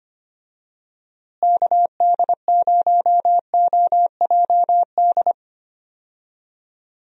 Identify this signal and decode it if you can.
Morse code, heard as a keyed tone, about 25 wpm, 700 Hz. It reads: KD0OJD
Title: USA Call Signs - 6 Characters - Repeated 2x - 25wpm